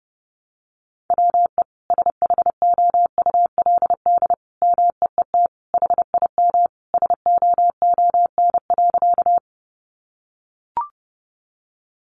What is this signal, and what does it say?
Morse code, heard as a keyed tone, about 30 wpm, 700 Hz. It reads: WE SHOULD MEET HIM SOON.
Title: Sentences from Top 1100 Words 30wpm